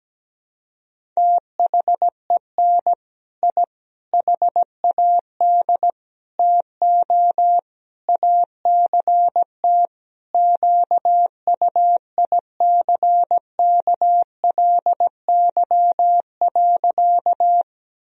Morse code, heard as a keyed tone, about 17 wpm, 700 Hz. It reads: THEN I HAD TO ACT QUICKLY.
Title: Sentences from Top 1000 Words - Review Part 1 - 17wpm